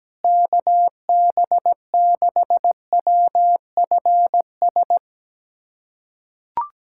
Morse code, heard as a keyed tone, about 17 wpm, 700 Hz. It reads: KB6WFS
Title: USA Callsign DE Callsign - 6 Characters - 17wpm